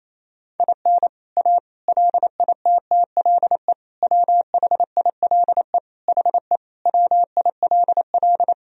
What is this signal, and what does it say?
Morse code, heard as a keyed tone, about 28 wpm, 700 Hz. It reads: IN A LITTLE WHILE HE WILL
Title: Sentences from Top 1000 Words - Review Part 1 - 28wpm